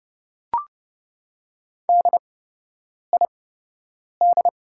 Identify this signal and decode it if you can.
Morse code, heard as a keyed tone, about 30 wpm, spaced as for 8 wpm, 700 Hz. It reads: DID
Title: 8x Character Spacing - Top 200 Words - 30wpm